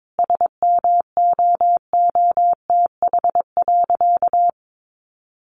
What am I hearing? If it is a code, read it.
Morse code, heard as a keyed tone, about 22 wpm, 700 Hz. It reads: SMOOTH.
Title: Sentences from Top 1400 Words 22wpm